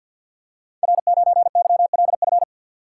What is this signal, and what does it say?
Morse code, heard as a keyed tone, about 51 wpm, 700 Hz. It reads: R9YLF
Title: Code Groups - 5 Letter:Numbers - 50wpm